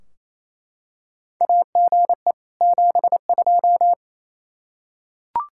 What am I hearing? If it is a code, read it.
Morse code, heard as a keyed tone, about 28 wpm, 700 Hz. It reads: AGE 72
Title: K1USN SST QSO Practice 28wpm